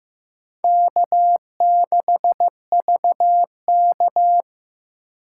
Morse code, heard as a keyed tone, about 15 wpm, 700 Hz. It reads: K6VK
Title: USA Call Signs - 4 Characters - 2x Character Spacing - 15wpm